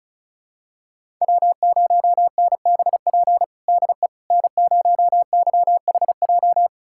Morse code, heard as a keyed tone, about 36 wpm, 700 Hz. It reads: W0NBP DE N0YHJ
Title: USA Callsign DE Callsign - 5 Characters - 35wpm